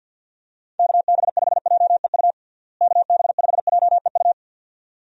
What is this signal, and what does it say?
Morse code, heard as a keyed tone, about 46 wpm, 700 Hz. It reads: KB5JEU KB5JEU
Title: USA Call Signs - 6 Characters - Repeated 2x - 50wpm